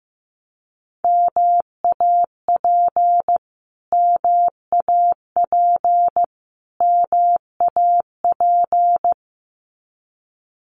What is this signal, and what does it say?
Morse code, heard as a keyed tone, about 15 wpm, 700 Hz. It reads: MAP MAP MAP
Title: Top 500 Words Repeated 15wpm